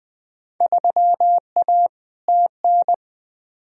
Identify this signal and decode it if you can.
Morse code, heard as a keyed tone, about 20 wpm, 700 Hz. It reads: 3A TN
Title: Field Day Exchanges - 20wpm